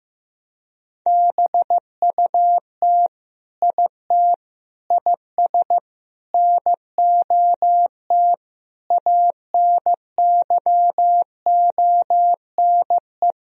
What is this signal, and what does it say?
Morse code, heard as a keyed tone, about 15 wpm, 700 Hz. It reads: BUT IT IS NOT ANYONE
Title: Sentences from Top 1000 Words - Review Part 4 - 15wpm